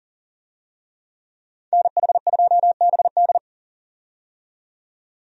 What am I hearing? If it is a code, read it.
Morse code, heard as a keyed tone, about 38 wpm, 700 Hz. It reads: NH2BD
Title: USA Call Signs - 5 Characters - 40wpm